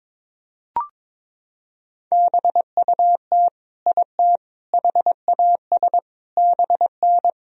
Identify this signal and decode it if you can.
Morse code, heard as a keyed tone, about 22 wpm, 700 Hz. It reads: BUT IT HAS BN
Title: Sentences from Top 100 Words and Abbreviations 22wpm